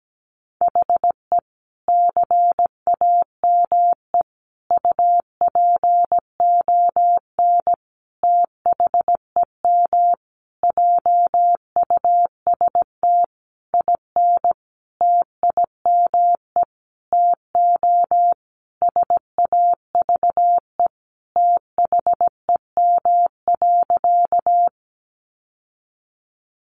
Morse code, heard as a keyed tone, about 17 wpm, 700 Hz. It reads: HE CAME UPON THEM JUST IN TIME TO SAVE THEM.
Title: Sentences from Top 1100 Words 17wpm